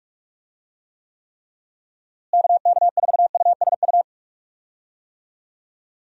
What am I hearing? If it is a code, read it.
Morse code, heard as a keyed tone, about 46 wpm, 700 Hz. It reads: KK4USU
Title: USA Call Signs - 6 Characters - Repeated 3x - 45wpm